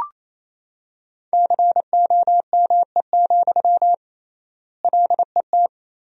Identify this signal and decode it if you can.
Morse code, heard as a keyed tone, about 28 wpm, 700 Hz. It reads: COME, LET
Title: 3x Word-Spacing - Sentences from Top 300 Words - 28wpm